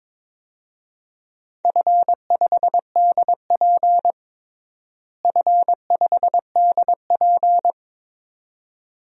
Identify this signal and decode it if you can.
Morse code, heard as a keyed tone, about 22 wpm, 700 Hz. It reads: F5DP F5DP
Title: DXCC-100 Call Signs - 4 Characters - Repeated 2x - 22wpm